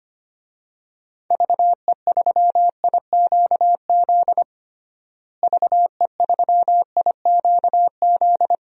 Morse code, heard as a keyed tone, about 25 wpm, 700 Hz. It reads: VE3IQZ VE3IQZ
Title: DXCC-100 Call Signs - 6 Characters - Repeated 3x - 25wpm